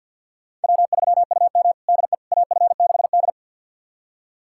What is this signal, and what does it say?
Morse code, heard as a keyed tone, about 49 wpm, 700 Hz. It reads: W3UM DE AF6D
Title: USA Callsign DE Callsign - 4 Characters - 50wpm